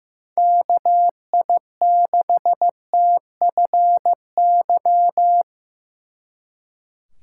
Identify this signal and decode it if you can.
Morse code, heard as a keyed tone, about 17 wpm, 700 Hz. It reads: KI6TFY
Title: Speed Racing - USA 2x3 Call Signs 15wpm